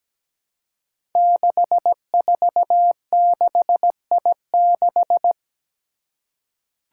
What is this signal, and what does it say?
Morse code, heard as a keyed tone, about 17 wpm, 700 Hz. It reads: 646I6
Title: Five Challenging Letter-Number Combinations 17wpm